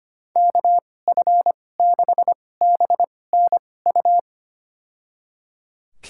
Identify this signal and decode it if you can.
Morse code, heard as a keyed tone, about 25 wpm, spaced as for 18 wpm, 700 Hz. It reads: KF6BNU
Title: USA Call Signs - 6 Characters - 2x Character Spacing - 25wpm